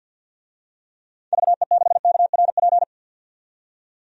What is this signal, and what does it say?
Morse code, heard as a keyed tone, about 47 wpm, 700 Hz. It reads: VE6KRP
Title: DXCC-100 Call Signs - 6 Characters - 50wpm